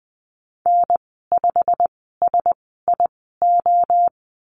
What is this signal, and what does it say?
Morse code, heard as a keyed tone, about 20 wpm, spaced as for 14 wpm, 700 Hz. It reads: N5SIO
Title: 140 Lesson 23g - Letter f - Call Signs 20wpm